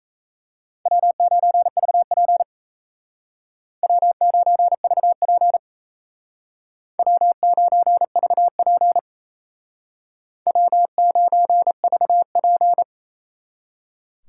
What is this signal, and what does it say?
Morse code, heard as a keyed tone, about 36 wpm, 700 Hz. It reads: W9VP W9VP W9VP W9VP